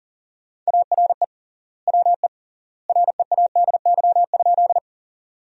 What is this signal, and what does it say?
Morse code, heard as a keyed tone, about 40 wpm, 700 Hz. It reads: ARE WE READY?
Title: 3x Word-Spacing - Sentences from Top 400 Words - 40wpm